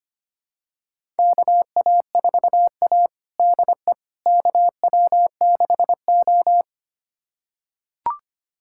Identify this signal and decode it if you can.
Morse code, heard as a keyed tone, about 25 wpm, 700 Hz. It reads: KA4A DE KW6O
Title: USA Callsign DE Callsign - 4 Characters - 25wpm